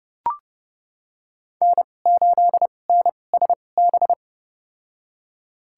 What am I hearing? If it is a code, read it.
Morse code, heard as a keyed tone, about 30 wpm, spaced as for 22 wpm, 700 Hz. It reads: N8NSB
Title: USA Call Signs - 5 Characters - 2x Character Spacing - 30wpm